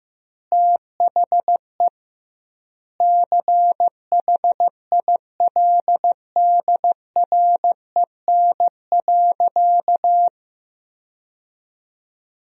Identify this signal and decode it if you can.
Morse code, heard as a keyed tone, about 15 wpm, 700 Hz. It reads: THE CHILDREN.
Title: 2x Word-Spacing - Sentences from Top 300 Words - 15wpm